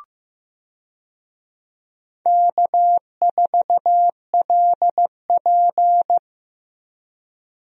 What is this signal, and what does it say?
Morse code, heard as a keyed tone, about 15 wpm, 700 Hz. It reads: K4LP
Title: USA Call Signs - 4 Characters - Repeated 2x - 15wpm